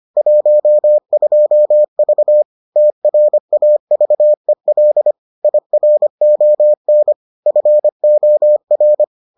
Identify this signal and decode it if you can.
Morse code, heard as a keyed tone, about 25 wpm, 600 Hz. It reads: 12V TRAVEL IRON FOR